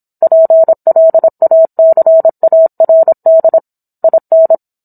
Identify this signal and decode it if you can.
Morse code, heard as a keyed tone, about 27 wpm, 650 Hz. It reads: PLACARD IN